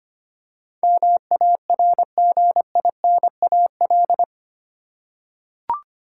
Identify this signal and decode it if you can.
Morse code, heard as a keyed tone, about 25 wpm, 700 Hz. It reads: MARGINAL
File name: Eight Letter Words - 25wpm